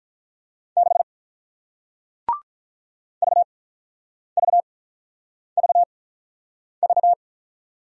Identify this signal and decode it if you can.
Morse code, heard as a keyed tone, about 48 wpm, spaced as for 9 wpm, 700 Hz. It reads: 6 VVVV